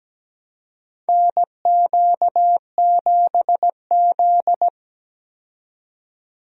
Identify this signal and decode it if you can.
Morse code, heard as a keyed tone, about 17 wpm, 700 Hz. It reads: NQ7Z
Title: USA Call Signs - 4 Characters - 17wpm